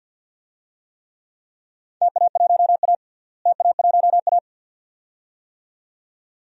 Morse code, heard as a keyed tone, about 51 wpm, 700 Hz. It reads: TA1A TA1A
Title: DXCC-100 Call Signs - 4 Characters - Repeated 2x - 50wpm